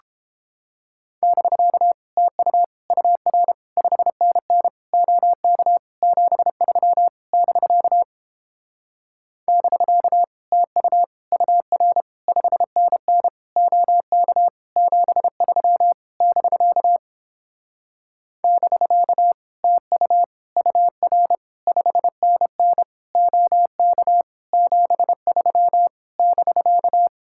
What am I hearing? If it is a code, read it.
Morse code, heard as a keyed tone, about 31 wpm, 700 Hz. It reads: <BK> TU UR 5NN OK 73 <BK> <BK> TU UR 5NN OK 73 <BK> <BK> TU UR 5NN OK 73 <BK>